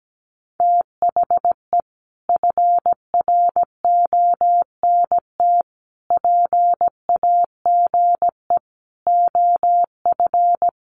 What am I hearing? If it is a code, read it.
Morse code, heard as a keyed tone, about 17 wpm, 700 Hz. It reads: THE FRONT PAGE OF
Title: Sets of 5 Words Encore 17wpm